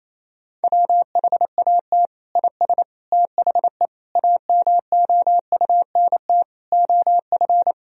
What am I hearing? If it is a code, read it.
Morse code, heard as a keyed tone, about 28 wpm, 700 Hz. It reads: WHAT IS THE AMOUNT OF
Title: Sentences from Top 1000 Words - Review Part 4 - 28wpm